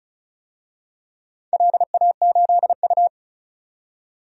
Morse code, heard as a keyed tone, about 36 wpm, 700 Hz. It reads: LA8U
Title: DXCC-100 Call Signs - 4 Characters - 35wpm